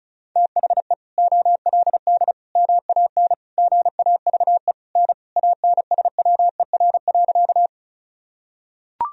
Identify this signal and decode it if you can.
Morse code, heard as a keyed tone, about 36 wpm, 700 Hz. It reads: THE OLD MAN GAVE N ANSWER.
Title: Sentences from Top 500 Words and Abbreviations 35wpm